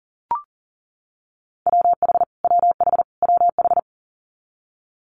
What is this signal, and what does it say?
Morse code, heard as a keyed tone, about 41 wpm, 700 Hz. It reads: WH WH WH
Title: Multi-letter Phonemes Repeated 3x 40wpm